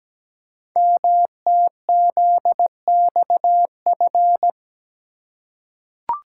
Morse code, heard as a keyed tone, about 17 wpm, 700 Hz. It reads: MTZXF
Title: Code Groups - 5 Letters - 17wpm